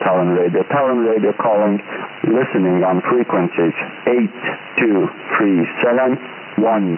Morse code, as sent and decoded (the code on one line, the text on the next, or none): none